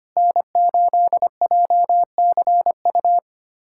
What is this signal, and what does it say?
Morse code, heard as a keyed tone, about 25 wpm, 700 Hz. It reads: N8JCU
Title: USA Callsign DE Callsign - 5 Characters - 25wpm